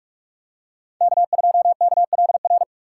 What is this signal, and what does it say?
Morse code, heard as a keyed tone, about 46 wpm, 700 Hz. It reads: K2KLR
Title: USA Call Signs - 5 Characters - 8x Character Spacing - 45wpm